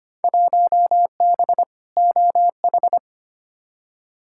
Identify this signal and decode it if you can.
Morse code, heard as a keyed tone, about 25 wpm, 700 Hz. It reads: 1B OH
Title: Field Day Exchanges - 25wpm